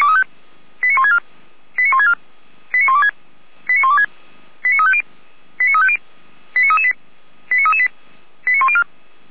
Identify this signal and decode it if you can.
Five-tone selective call (ZVEI), heard as quick runs of five tones, tones of about 70 ms. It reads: 89164 89164 89126 89126 89350 89350 89338 89338 89104
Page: /455/SDR/ZVEI/89701/2023.07.05/